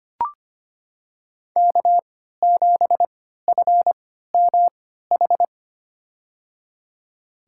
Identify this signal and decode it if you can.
Morse code, heard as a keyed tone, about 25 wpm, spaced as for 14 wpm, 700 Hz. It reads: K7FMH
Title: USA Call Signs - 5 Characters - 3x Character Spacing - 25wpm